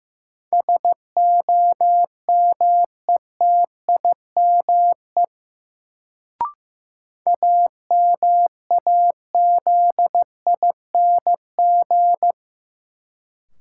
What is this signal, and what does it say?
Morse code, heard as a keyed tone, about 15 wpm, 700 Hz. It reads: SOMETIME AMAZING
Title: Sentences from Top 1500 Words - New Word Drill Alt - 15wpm